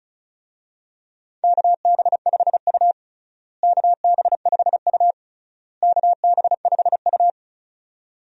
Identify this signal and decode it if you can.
Morse code, heard as a keyed tone, about 36 wpm, 700 Hz. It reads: KB5U KB5U KB5U
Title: USA Call Signs - 4 Characters - Repeated 3x - 35wpm